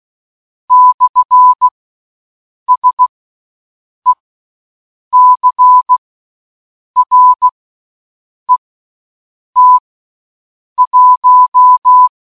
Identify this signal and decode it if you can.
Morse code, heard as a keyed tone, about 16 wpm, spaced as for 7 wpm, 1000 Hz. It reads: /SECRET1
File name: morse-code-button